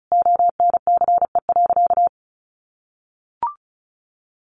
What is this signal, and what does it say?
Morse code, heard as a keyed tone, about 34 wpm, 700 Hz. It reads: ONCE.
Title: Sentences from Top 500 Words 35wpm